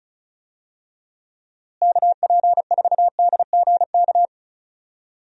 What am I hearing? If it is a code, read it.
Morse code, heard as a keyed tone, about 34 wpm, 700 Hz. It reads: KP4DGK
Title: USA Call Signs - 6 Characters - Repeated 2x - 35wpm